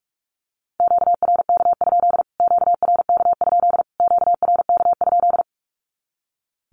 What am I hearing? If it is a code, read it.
Morse code, heard as a keyed tone, about 45 wpm, 700 Hz. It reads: QRK? QRK? QRK?
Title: Q-Codes Repeated 3x 45wpm